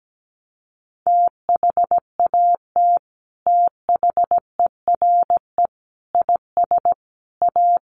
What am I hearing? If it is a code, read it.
Morse code, heard as a keyed tone, about 17 wpm, 700 Hz. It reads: THAT THERE IS A
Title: Sets of 4 Words 17wpm